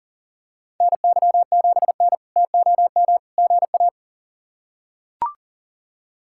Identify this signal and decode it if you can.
Morse code, heard as a keyed tone, about 40 wpm, 700 Hz. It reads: NY7N TOM GA
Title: K1USN SST QSO Practice - 40wpm